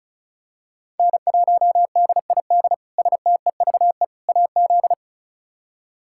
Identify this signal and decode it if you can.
Morse code, heard as a keyed tone, about 36 wpm, 700 Hz. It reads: N1DID STEVE AZ
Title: K1USN SST QSO Practice - 35wpm